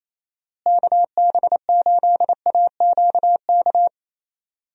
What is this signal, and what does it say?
Morse code, heard as a keyed tone, about 27 wpm, 700 Hz. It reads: KB8AQK
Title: USA Call Signs - 6 Characters - Repeated 3x - 28wpm